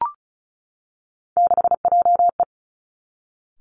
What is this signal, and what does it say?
Morse code, heard as a keyed tone, about 36 wpm, 700 Hz. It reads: 6JE
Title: Three Challenging Letter-Number Combinations 35wpm